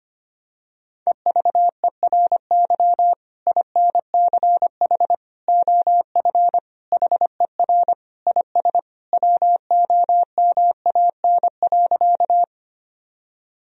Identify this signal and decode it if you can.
Morse code, heard as a keyed tone, about 25 wpm, 700 Hz. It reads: EVERY INCH OF HER IS WOMAN.
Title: Sentences from Top 1000 Words - Review Part 3 - 25wpm